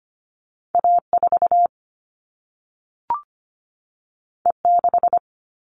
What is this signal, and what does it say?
Morse code, heard as a keyed tone, about 25 wpm, 700 Hz. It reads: A 4 E 6
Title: All Cut Numbers - 2-Digit 25wpm